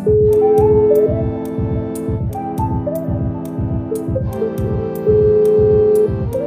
Ringback tone (Europe, 425 Hz), with noise.